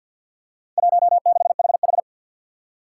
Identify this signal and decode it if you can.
Morse code, heard as a keyed tone, about 48 wpm, 700 Hz. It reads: 16HH
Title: Four Challenging Letter-Number Combinations 50wpm